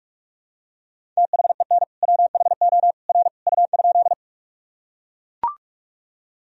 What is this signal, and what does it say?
Morse code, heard as a keyed tone, about 43 wpm, 700 Hz. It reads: THEN WHO R U?
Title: Sentences from Top 100 Words and Abbreviations 45wpm